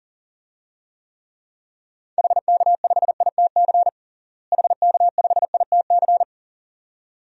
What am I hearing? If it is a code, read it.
Morse code, heard as a keyed tone, about 40 wpm, 700 Hz. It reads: HK5ITC HK5ITC
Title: DXCC-100 Call Signs - 6 Characters - Repeated 2x - 40wpm